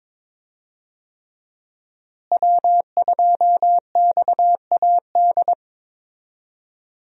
Morse code, heard as a keyed tone, about 22 wpm, 690 Hz. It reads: W2XAD
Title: USA Call Signs - 5 Characters - Repeated 3x - 22wpm